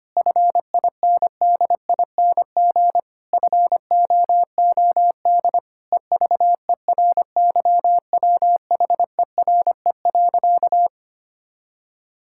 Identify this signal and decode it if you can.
Morse code, heard as a keyed tone, about 25 wpm, 700 Hz. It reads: FINDING FOOD EVERYWHERE.
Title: Sentences from Top 1600 Words 25wpm